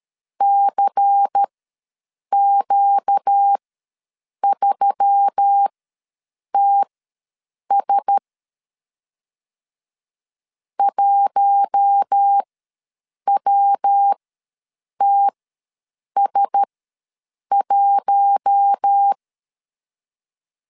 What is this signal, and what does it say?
Morse code, heard as a keyed tone, about 13 wpm, spaced as for 7 wpm, 800 Hz. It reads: CQ3TS 1WTS1
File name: LEZIONI / 5 LBVC / 5 LBVC ws 12wpm 3159